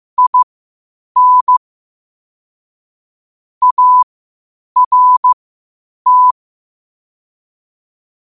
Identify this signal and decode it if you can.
Morse code, heard as a keyed tone, about 15 wpm, spaced as for 8 wpm, 1000 Hz. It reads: IN ART
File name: qotdfiles / 8wpm